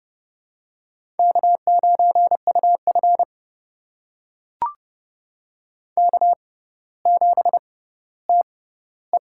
Morse code, heard as a keyed tone, about 30 wpm, spaced as for 15 wpm, 700 Hz. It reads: K9UF K7TE
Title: USA Call Signs - 4 Characters - 6x Character Spacing - 30wpm